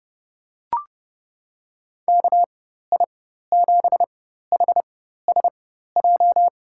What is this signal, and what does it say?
Morse code, heard as a keyed tone, about 30 wpm, spaced as for 14 wpm, 700 Hz. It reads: KI7HSJ
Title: USA Call Signs - 6 Characters - 4x Character Spacing - 30wpm